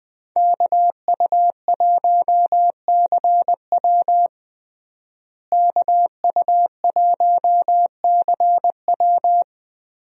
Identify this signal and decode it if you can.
Morse code, heard as a keyed tone, about 20 wpm, 700 Hz. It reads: KU1CW KU1CW
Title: USA Call Signs - 5 Characters - Repeated 3x - 20wpm